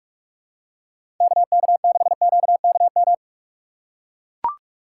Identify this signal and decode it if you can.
Morse code, heard as a keyed tone, about 44 wpm, 700 Hz. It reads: KK6QKM
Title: USA Call Signs - 6 Characters - 6x Character Spacing - 45wpm